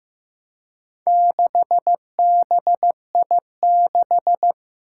Morse code, heard as a keyed tone, about 15 wpm, 700 Hz. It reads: 6BI6
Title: Four Challenging Letter-Number Combinations 15wpm